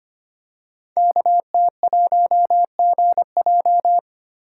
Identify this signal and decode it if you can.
Morse code, heard as a keyed tone, about 25 wpm, 700 Hz. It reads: KT1GJ
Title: Code Groups - 5 Letter:Numbers - 25wpm